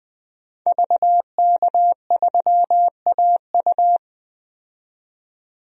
Morse code, heard as a keyed tone, about 20 wpm, 700 Hz. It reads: VK3AU
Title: DXCC-100 Call Signs - 5 Characters - 20wpm